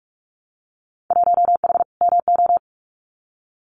Morse code, heard as a keyed tone, about 46 wpm, 700 Hz. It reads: 1H MO
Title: Winter Field Day Exchanges 45wpm